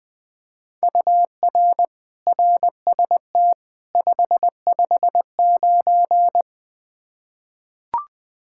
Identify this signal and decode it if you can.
Morse code, heard as a keyed tone, about 20 wpm, 700 Hz. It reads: UR RST 559
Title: Practice QSO - Exchange 1 - Beginning - 20wpm